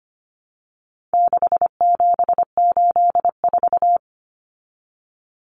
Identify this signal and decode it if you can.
Morse code, heard as a keyed tone, about 25 wpm, 700 Hz. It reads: 6784
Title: Common Cut Numbers - 4-Digit 25wpm